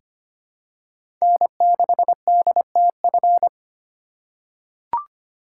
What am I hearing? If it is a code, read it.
Morse code, heard as a keyed tone, about 25 wpm, 700 Hz. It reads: N6DTF
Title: USA Call Signs - 4x Character Spacing - 25wpm